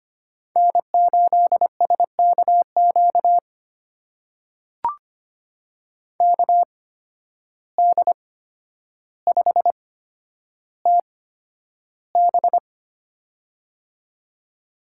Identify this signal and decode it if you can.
Morse code, heard as a keyed tone, about 25 wpm, spaced as for 11 wpm, 700 Hz. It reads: N8SKQ KD5TB
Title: USA Call Signs - 5 Characters - 8x Character Spacing - 25wpm